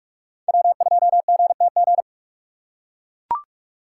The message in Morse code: W2GTG

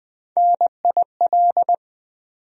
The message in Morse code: NIL

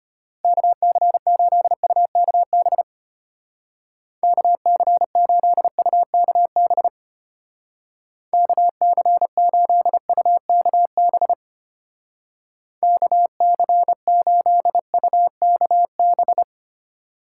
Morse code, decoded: KC8UKB KC8UKB KC8UKB KC8UKB

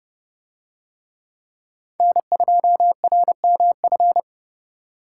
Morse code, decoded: N2RMF

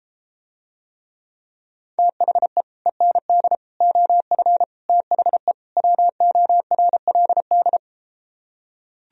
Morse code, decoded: THE END OF THE WORLD

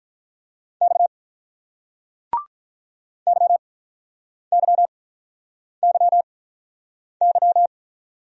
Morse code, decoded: X YYYY

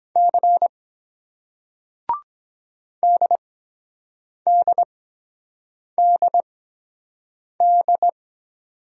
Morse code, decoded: C DDDD

